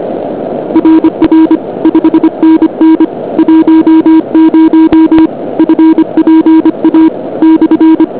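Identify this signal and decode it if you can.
Morse code, heard as a keyed tone, about 25 wpm, 320 Hz. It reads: RR 5NN 10 FPA /
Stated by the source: Amateur Morse contacts from the QRP 2008 'QRPists' ball' event with OK1AIJ, OK2BME, OK1DAM and OK1HSK, low-power QRP stations in less than ideal conditions.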